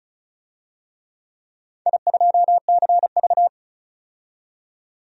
Morse code, heard as a keyed tone, about 34 wpm, 700 Hz. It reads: I2CV
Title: DXCC-100 Call Signs - 4 Characters - 35wpm